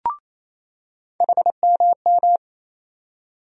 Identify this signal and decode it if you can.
Morse code, heard as a keyed tone, about 28 wpm, 700 Hz. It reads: HMM
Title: USA Call Sign Postfixes - 3 Letters 28wpm